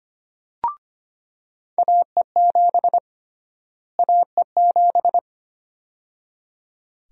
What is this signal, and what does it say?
Morse code, heard as a keyed tone, about 25 wpm, 700 Hz. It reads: AE7 AE7
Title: USA Call Sign Prefixes - Repeated 2x - 25wpm